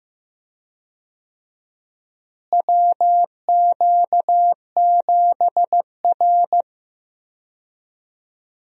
Morse code, heard as a keyed tone, about 15 wpm, 700 Hz. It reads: WQ7R
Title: USA Call Signs - 4 Characters - Repeated 3x - 15wpm